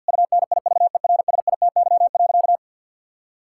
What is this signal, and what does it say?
Morse code, heard as a keyed tone, about 50 wpm, 700 Hz. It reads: UNIVERSITY.
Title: Sentences from Top 1000 Words - Review Part 6 - 50wpm